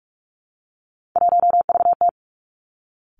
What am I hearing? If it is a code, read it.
Morse code, heard as a keyed tone, about 46 wpm, 700 Hz. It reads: 1VT
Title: Three Challenging Letter-Number Combinations 45wpm